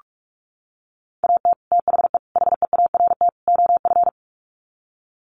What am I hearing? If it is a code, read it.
Morse code, heard as a keyed tone, about 44 wpm, 700 Hz. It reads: AT THE HEART OF
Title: Sets of 4 Words 45wpm